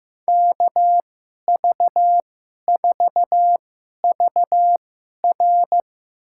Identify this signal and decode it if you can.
Morse code, heard as a keyed tone, about 15 wpm, spaced as for 11 wpm, 700 Hz. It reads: KV4VR
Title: USA Call Signs - 2x Character Spacing - 15wpm